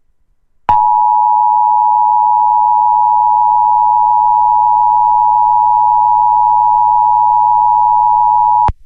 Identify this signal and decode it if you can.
Emergency Alert System (SAME) tones (the 853 and 960 Hz attention tone). It is heard from 0.7 to 8.1 s.